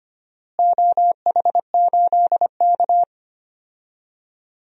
Morse code, heard as a keyed tone, about 25 wpm, 700 Hz. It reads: OH8K
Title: DXCC-100 Call Signs - 4 Characters - Repeated 2x - 25wpm